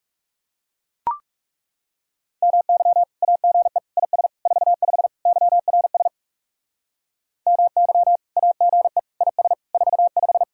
Morse code, heard as a keyed tone, about 36 wpm, 700 Hz. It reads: MY AGE IS 45 YRS MY AGE IS 45
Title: Speed-Racing Common QSO Phrases - 30wpm